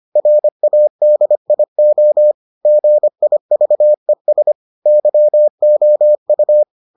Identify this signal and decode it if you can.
Morse code, heard as a keyed tone, about 25 wpm, 600 Hz. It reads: RADIO GIVES YOU